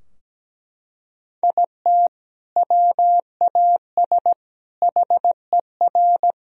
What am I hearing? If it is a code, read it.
Morse code, heard as a keyed tone, about 17 wpm, 700 Hz. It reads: IT WAS HER
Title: Sentences from Top 1000 Words - Review Part 4 - 17wpm